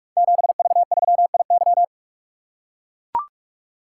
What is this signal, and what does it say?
Morse code, heard as a keyed tone, about 44 wpm, 700 Hz. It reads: 7V3IY
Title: Code Groups - 5 Letter:Numbers - 45wpm